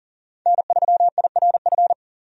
Morse code, heard as a keyed tone, about 37 wpm, 700 Hz. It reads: N3IRF
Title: USA Call Signs - 5 Characters - 40wpm